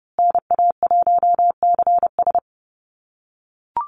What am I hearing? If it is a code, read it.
Morse code, heard as a keyed tone, about 30 wpm, 700 Hz. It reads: NA1CS
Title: USA Call Signs Repeated 2x 30wpm